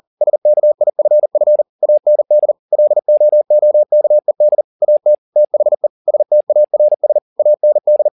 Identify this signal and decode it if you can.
Morse code, heard as a keyed tone, about 40 wpm, 600 Hz. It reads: SKIFF AND LOOKED AT THE STARS AND